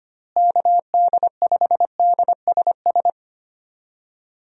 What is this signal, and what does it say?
Morse code, heard as a keyed tone, about 25 wpm, 700 Hz. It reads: KD5DSS
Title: USA Call Signs - 6 Characters - Repeated 2x - 25wpm